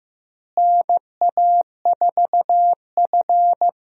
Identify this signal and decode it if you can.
Morse code, heard as a keyed tone, about 15 wpm, 700 Hz. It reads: NA4F
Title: USA Call Signs - 4 Characters - Repeated 2x - 15wpm